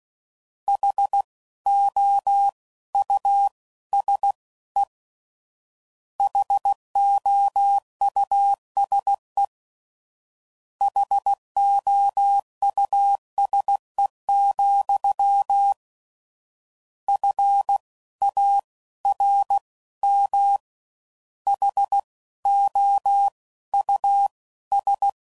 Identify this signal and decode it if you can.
Morse code, heard as a keyed tone, about 16 wpm, 800 Hz. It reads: H O U S E HOUSE HOUSE, F A R M H O U S
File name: Suffixes / General / Houses